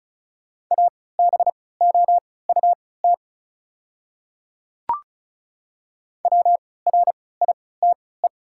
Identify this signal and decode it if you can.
Morse code, heard as a keyed tone, about 36 wpm, spaced as for 20 wpm, 690 Hz. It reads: ABOUT WRITE
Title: 3x Character Spacing - Top 100 Words - 35wpm